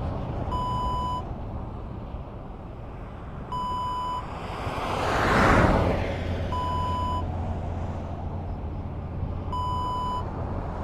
There is no code.